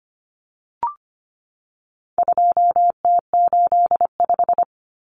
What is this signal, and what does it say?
Morse code, heard as a keyed tone, about 25 wpm, 700 Hz. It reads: 2T85
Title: Common Cut Numbers - 4-Digit 25wpm